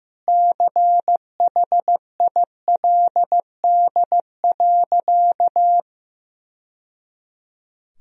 Morse code, heard as a keyed tone, about 15 wpm, 700 Hz. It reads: CHILD.
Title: Sentences from Top 1000 Words - Review Part 4 - 15wpm